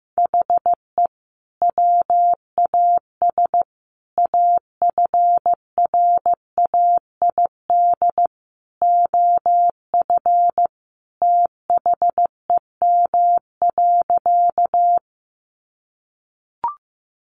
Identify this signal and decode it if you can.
Morse code, heard as a keyed tone, about 15 wpm, 700 Hz. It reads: HE WAS AFRAID OF THEM.
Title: Sentences from Top 1500 Words 15wpm